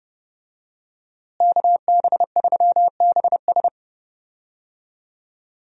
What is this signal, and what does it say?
Morse code, heard as a keyed tone, about 31 wpm, 700 Hz. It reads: KB3BS
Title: USA Call Signs - 5 Characters - Repeated 3x - 30wpm